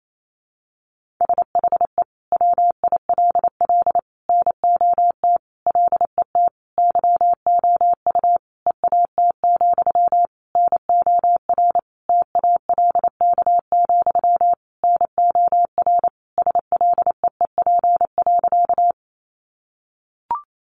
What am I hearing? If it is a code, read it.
Morse code, heard as a keyed tone, about 28 wpm, 700 Hz. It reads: SHE WILL NOT LET YOU EAT, NOR TALK, NOR SLEEP.
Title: Sentences from Top 1500 Words 28wpm